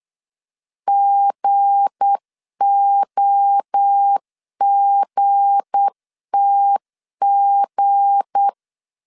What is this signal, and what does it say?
Morse code, heard as a keyed tone, about 8 wpm, 800 Hz. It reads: GOGTG
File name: LEZIONI / 7 GF9Z / 7 G 8wpm 3118